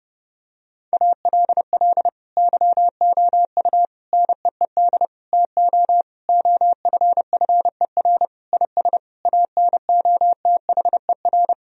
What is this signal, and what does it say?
Morse code, heard as a keyed tone, about 30 wpm, 700 Hz. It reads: ALL YOU NEED TO OFFER IS ANOTHER
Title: Sentences from Top 1000 Words - Review Part 4 - 30wpm